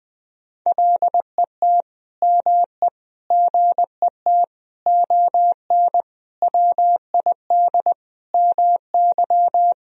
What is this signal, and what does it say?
Morse code, heard as a keyed tone, about 20 wpm, 700 Hz. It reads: LET ME GET ON WID MY
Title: Sentences from Top 300 Words and Abbreviations 20wpm